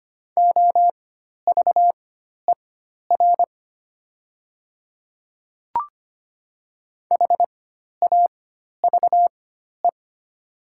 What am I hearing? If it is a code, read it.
Morse code, heard as a keyed tone, about 25 wpm, spaced as for 12 wpm, 700 Hz. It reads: OVER HAVE
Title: 4x Character Spacing - Top 100 Words - 25wpm